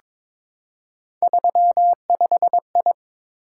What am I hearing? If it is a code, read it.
Morse code, heard as a keyed tone, about 22 wpm, 700 Hz. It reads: 35I
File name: practice / 22wpm / Three Letter-Number Combinations - 22wpm